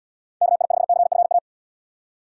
Morse code, heard as a keyed tone, about 78 wpm, 700 Hz. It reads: KI5CFA